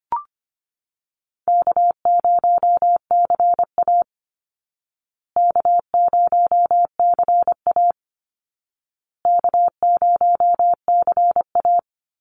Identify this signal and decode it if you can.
Morse code, heard as a keyed tone, about 25 wpm, 700 Hz. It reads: K0CA K0CA K0CA
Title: Call Signs Repeated 2x - 25wpm